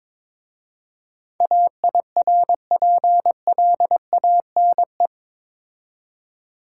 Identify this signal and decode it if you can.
Morse code, heard as a keyed tone, about 25 wpm, 700 Hz. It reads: AIRPLANE
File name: Easy Words - Speed-Racing with Hints - 22wpm